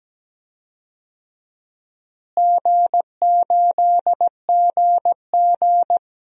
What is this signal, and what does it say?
Morse code, heard as a keyed tone, about 17 wpm, 700 Hz. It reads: G8GG
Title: DXCC-100 Call Signs - 4 Characters - Repeated 2x - 17wpm